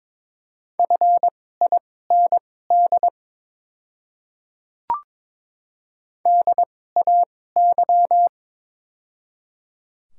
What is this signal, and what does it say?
Morse code, heard as a keyed tone, about 22 wpm, spaced as for 16 wpm, 700 Hz. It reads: FIND DAY
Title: 2x Character Spacing - Top 100 Words - 22wpm